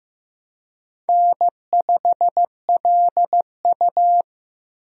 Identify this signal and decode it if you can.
Morse code, heard as a keyed tone, about 15 wpm, 700 Hz. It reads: N5LU
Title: USA Call Signs - 4 Characters - Repeated 3x - 15wpm